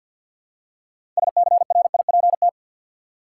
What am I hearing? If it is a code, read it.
Morse code, heard as a keyed tone, about 48 wpm, 700 Hz. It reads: SCRIPT
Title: Six Letter Words - 50wpm